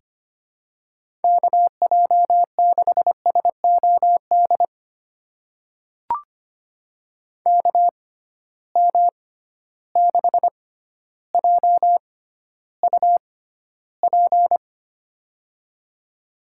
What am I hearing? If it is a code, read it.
Morse code, heard as a keyed tone, about 25 wpm, spaced as for 13 wpm, 700 Hz. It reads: KJ6SOD KM6JUP
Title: USA Call Signs - 6 Characters - 6x Character Spacing - 25wpm